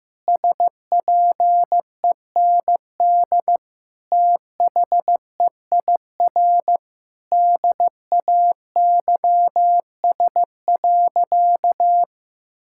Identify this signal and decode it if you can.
Morse code, heard as a keyed tone, about 16 wpm, 700 Hz. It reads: SPEND THEIR DAYS.
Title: Sentences from Top 1000 Words - Review Part 5 - 15wpm